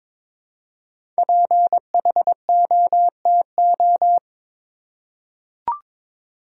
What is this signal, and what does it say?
Morse code, heard as a keyed tone, about 23 wpm, 700 Hz. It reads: PHOTO
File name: 5 Letter Words - First 3 Letters Given - 22wpm